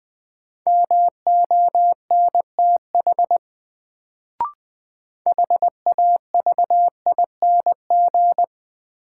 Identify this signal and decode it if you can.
Morse code, heard as a keyed tone, about 20 wpm, 700 Hz. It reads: MONTH HAVING
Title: Sets of 5 Words Encore - New Word Drill - 20wpm